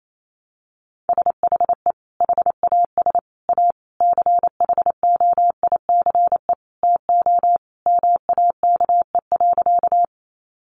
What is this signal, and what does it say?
Morse code, heard as a keyed tone, about 28 wpm, 700 Hz. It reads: SHE HAS A CHOICE TO MAKE.
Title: Sentences from Top 900 Words 28wpm